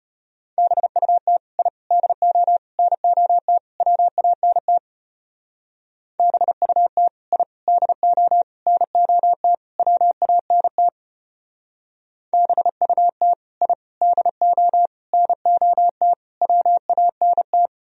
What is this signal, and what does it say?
Morse code, heard as a keyed tone, about 31 wpm, 700 Hz. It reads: BUT I DO NOT WANT BUT I DO NOT WANT BUT I DO NOT WANT
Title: Speed-Racing - Sets of 5 Words 25wpm